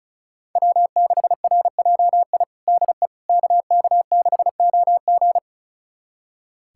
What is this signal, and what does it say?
Morse code, heard as a keyed tone, about 36 wpm, 700 Hz. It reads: W6RJI DE KK6OG
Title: USA Callsign DE Callsign - 5 Characters - 35wpm